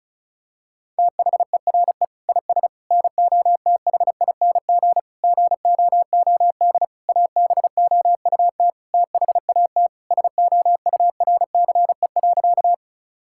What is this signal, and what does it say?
Morse code, heard as a keyed tone, about 36 wpm, 700 Hz. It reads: THERE IS NOTHING GOOD ABOUT THAT SOURCE.
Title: Sentences from Top 1000 Words - Review Part 5 - 35wpm